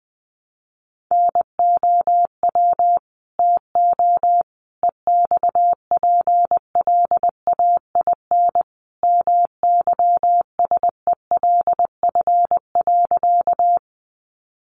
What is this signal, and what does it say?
Morse code, heard as a keyed tone, about 19 wpm, 700 Hz. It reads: NOW TO EXPLAIN MYSELF.
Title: Sentences from Top 1300 Words 20wpm